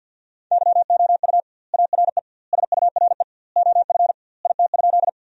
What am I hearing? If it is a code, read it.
Morse code, heard as a keyed tone, about 50 wpm, 700 Hz. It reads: YOU ARE SURE OF IT?
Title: 2x Word-Spacing - Sentences from Top 300 Words - 50wpm